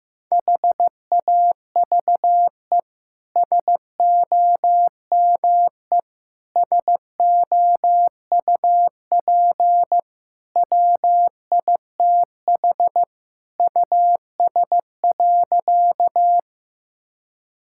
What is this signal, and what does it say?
Morse code, heard as a keyed tone, about 16 wpm, 700 Hz. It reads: HAVE SOME SOUP WITH US.
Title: Sentences from Top 1000 Words - Review Part 6 - 15wpm